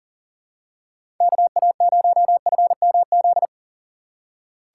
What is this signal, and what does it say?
Morse code, heard as a keyed tone, about 40 wpm, 700 Hz. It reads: KA0FMZ
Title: USA Call Signs - 6 Characters - 3x Character Spacing - 40wpm